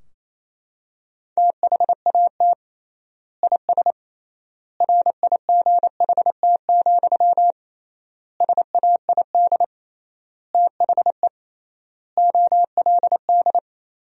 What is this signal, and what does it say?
Morse code, heard as a keyed tone, about 28 wpm, 700 Hz. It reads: THAT IS RIGHT, SAID THE OLD
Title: 3x Word-Spacing - Sentences from Top 200 Words - 28wpm